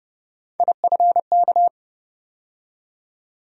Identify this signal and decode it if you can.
Morse code, heard as a keyed tone, about 30 wpm, 700 Hz. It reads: IFK
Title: USA Call Sign Postfixes - 3 Letters 30wpm